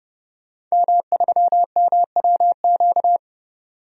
Morse code, heard as a keyed tone, about 31 wpm, 700 Hz. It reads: M3MWQ
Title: DXCC-100 Call Signs - 5 Characters - 30wpm